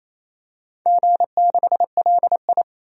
Morse code, heard as a keyed tone, about 27 wpm, 690 Hz. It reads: G6LI
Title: DXCC-100 Call Signs - 4 Characters - 28wpm